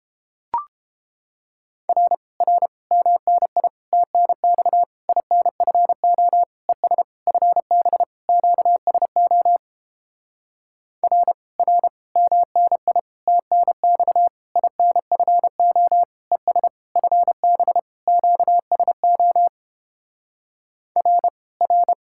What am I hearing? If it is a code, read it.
Morse code, heard as a keyed tone, about 29 wpm, 700 Hz. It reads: R R MNI TNX INFO ES FB QSO R R MNI TNX INFO ES FB QSO R R